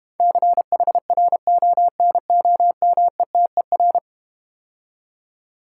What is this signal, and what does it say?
Morse code, heard as a keyed tone, about 32 wpm, 690 Hz. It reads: CHRONOMETER